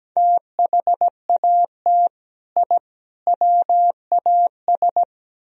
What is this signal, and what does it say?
Morse code, heard as a keyed tone, about 19 wpm, 700 Hz. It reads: THAT I WAS
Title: Speed-Racing - Sets of 3 Words 17wpm